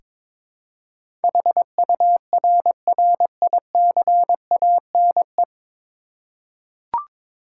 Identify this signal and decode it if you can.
Morse code, heard as a keyed tone, about 22 wpm, 700 Hz. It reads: HURRICANE
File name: practice / 22wpm / 9 Letter Words - First 3 Letters Given - 22wpm